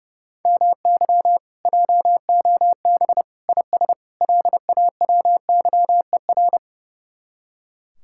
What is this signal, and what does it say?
Morse code, heard as a keyed tone, about 37 wpm, 700 Hz. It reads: MY JOB IS LAWYER
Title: Speed-Racing Common QSO Phrases - 30wpm